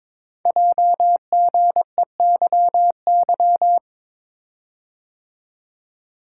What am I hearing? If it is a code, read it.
Morse code, heard as a keyed tone, about 22 wpm, 700 Hz. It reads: JGEYY
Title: Code Groups - 5 Letters - 22wpm